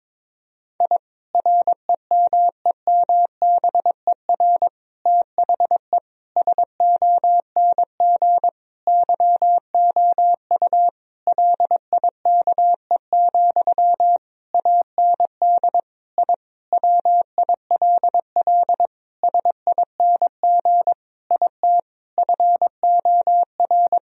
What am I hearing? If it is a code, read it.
Morse code, heard as a keyed tone, about 22 wpm, 700 Hz. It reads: I REMEMBER THE SONG YOU LIKE, AND I WILL SING IT FOR
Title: Sentences from Top 1000 Words - Review Part 2 - 22wpm